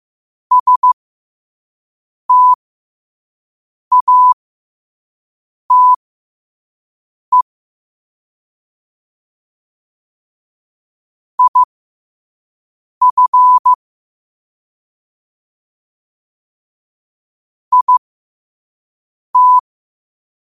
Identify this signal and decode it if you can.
Morse code, heard as a keyed tone, about 15 wpm, spaced as for 5 wpm, 1000 Hz. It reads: STATE IF IT
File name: qotdfiles / 5wpm